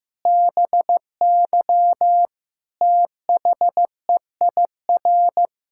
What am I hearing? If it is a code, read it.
Morse code, heard as a keyed tone, about 16 wpm, 700 Hz. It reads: BY THEIR
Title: Pangram Challenge v2 - 15wpm